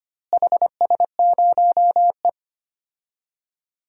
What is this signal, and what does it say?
Morse code, heard as a keyed tone, about 25 wpm, 700 Hz. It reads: HS0E
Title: DXCC-100 Call Signs - 4 Characters - Repeated 3x - 25wpm